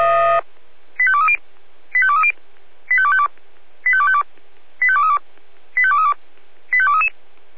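Five-tone selective call (ZVEI), heard as quick runs of five tones, tones of about 70 ms. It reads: 85230 85230 85252 85252 85232 85232 85233
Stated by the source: ZVEI five-tone call received by SDR.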